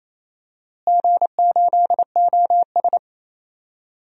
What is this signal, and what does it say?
Morse code, heard as a keyed tone, about 28 wpm, 700 Hz. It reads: G8OS
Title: DXCC-100 Call Signs - 4 Characters - Repeated 2x - 28wpm